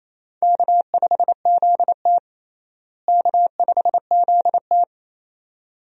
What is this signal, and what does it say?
Morse code, heard as a keyed tone, about 28 wpm, 700 Hz. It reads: K5ZT K5ZT
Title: USA Call Signs - 4 Characters - Repeated 2x - 28wpm